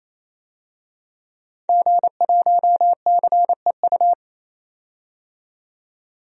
Morse code, heard as a keyed tone, about 28 wpm, 700 Hz. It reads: G1CEU
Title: DXCC-100 Call Signs - 5 Characters - 28wpm